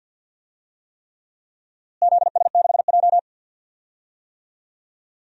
Morse code, heard as a keyed tone, about 47 wpm, 700 Hz. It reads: ZS6J
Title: DXCC-100 Call Signs - 4 Characters - Repeated 3x - 50wpm